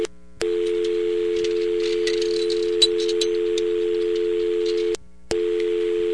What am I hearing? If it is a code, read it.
Dial tone (North America), 350 + 440 Hz.